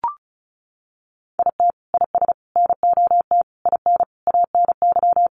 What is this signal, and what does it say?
Morse code, heard as a keyed tone, about 34 wpm, 700 Hz. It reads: IT IS NOT IN ANY
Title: Sentences from Top 1200 Words 35wpm